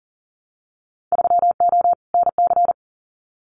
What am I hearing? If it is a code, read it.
Morse code, heard as a keyed tone, about 41 wpm, 700 Hz. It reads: 3O NC
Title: Winter Field Day Exchanges 40wpm